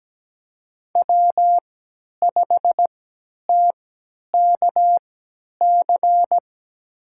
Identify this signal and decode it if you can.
Morse code, heard as a keyed tone, about 17 wpm, spaced as for 10 wpm, 700 Hz. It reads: W5TKC
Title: USA Call Signs - 5 Characters - 3x Character Spacing - 17wpm